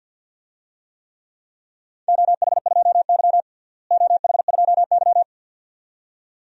Morse code, heard as a keyed tone, about 45 wpm, 700 Hz. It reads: OH2Y OH2Y
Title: DXCC-100 Call Signs - 4 Characters - Repeated 2x - 50wpm